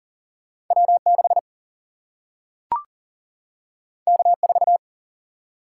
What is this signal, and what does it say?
Morse code, heard as a keyed tone, about 40 wpm, 700 Hz. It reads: W 6 K 4
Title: Single-Letter Single-Number 40wpm